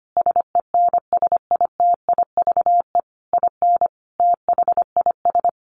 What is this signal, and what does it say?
Morse code, heard as a keyed tone, about 25 wpm, 700 Hz. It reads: SENSITIVE IN THIS
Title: Sentences from Top 1400 Words 25wpm